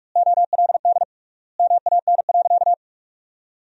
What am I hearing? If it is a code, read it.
Morse code, heard as a keyed tone, about 46 wpm, 700 Hz. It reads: OLD MAN.
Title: 3x Word-Spacing - Sentences from Top 500 Words - 45wpm